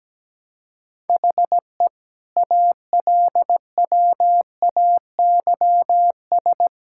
Morse code, heard as a keyed tone, about 17 wpm, 700 Hz. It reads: HE ALWAYS
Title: Sentences from Top 1600 Words 17wpm